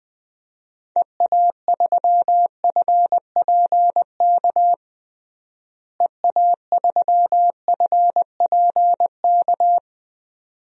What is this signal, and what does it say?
Morse code, heard as a keyed tone, about 20 wpm, 700 Hz. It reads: EA3FPK EA3FPK
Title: DXCC-100 Call Signs - 6 Characters - Repeated 2x - 20wpm